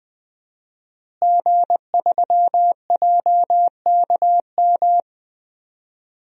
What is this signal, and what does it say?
Morse code, heard as a keyed tone, about 20 wpm, 700 Hz. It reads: G3JKM
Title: DXCC-100 Call Signs - 5 Characters - Repeated 2x - 20wpm